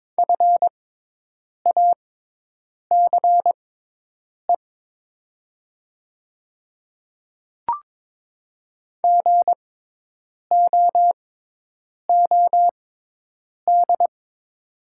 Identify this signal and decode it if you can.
Morse code, heard as a keyed tone, about 22 wpm, spaced as for 8 wpm, 700 Hz. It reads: FACE GOOD
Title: 6x Character Spacing - Top 300 Words - 22wpm